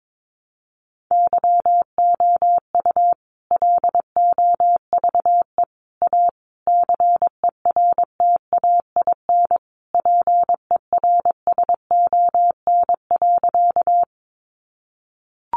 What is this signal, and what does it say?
Morse code, heard as a keyed tone, about 22 wpm, 700 Hz. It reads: YOU LOVE A CERTAIN PERSON.
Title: Sentences from Top 500 Words 22wpm